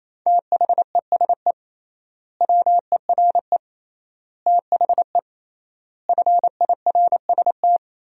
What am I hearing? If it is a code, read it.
Morse code, heard as a keyed tone, about 27 wpm, 700 Hz. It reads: THESE WERE THE FIRST
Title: 3x Word-Spacing - Sentences from Top 300 Words - 28wpm